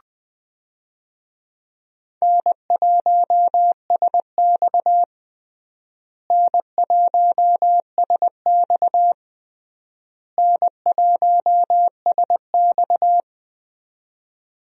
Morse code, heard as a keyed tone, about 19 wpm, 700 Hz. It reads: N1SX N1SX N1SX
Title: USA Call Signs - 4 Characters - Repeated 3x - 20wpm